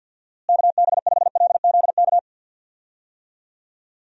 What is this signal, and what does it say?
Morse code, heard as a keyed tone, about 46 wpm, 700 Hz. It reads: KB5LZK
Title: USA Call Signs - 6 Characters - Repeated 2x - 50wpm